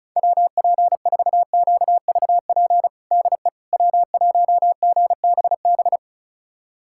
Morse code, heard as a keyed tone, about 34 wpm, 700 Hz. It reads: WP4QVP DE W1GBB
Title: USA Callsign DE Callsign 35wpm